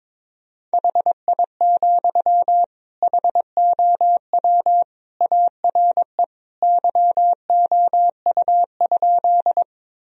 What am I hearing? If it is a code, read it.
Morse code, heard as a keyed tone, about 21 wpm, 700 Hz. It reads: HI, HOW ARE YOU?
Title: Sentences from Top 1000 Words - Review Part 3 - 22wpm